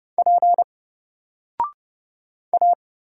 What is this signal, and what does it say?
Morse code, heard as a keyed tone, about 30 wpm, 700 Hz. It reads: P A
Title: Single Character (Letter:Number:Punctuation:Prosign) 30wpm